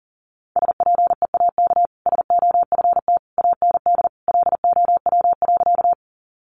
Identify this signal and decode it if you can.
Morse code, heard as a keyed tone, about 40 wpm, 700 Hz. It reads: SPEAK SOFT AND LOW.
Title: Sentences from Top 1500 Words 40wpm